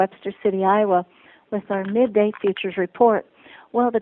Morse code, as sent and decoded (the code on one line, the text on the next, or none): none